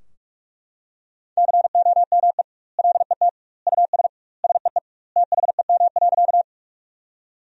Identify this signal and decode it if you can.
Morse code, heard as a keyed tone, about 41 wpm, 700 Hz. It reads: COME LET US SEE THEM.
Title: 2x Word-Spacing - Sentences from Top 300 Words - 45wpm